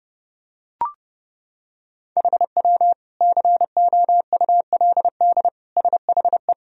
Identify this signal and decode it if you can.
Morse code, heard as a keyed tone, about 32 wpm, 700 Hz. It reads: HW COULD SHE
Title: Sentences from Top 200 Words and Abbreviations 30wpm